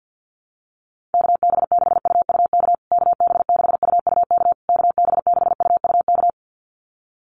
Morse code, heard as a keyed tone, about 50 wpm, 700 Hz. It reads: KB6UUK KB6UUK KB6UUK
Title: USA Call Signs Repeated 3x 50wpm